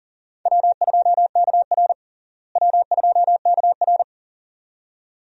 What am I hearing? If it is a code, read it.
Morse code, heard as a keyed tone, about 40 wpm, 700 Hz. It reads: W2KR W2KR
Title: USA Call Signs - 4 Characters - Repeated 2x - 40wpm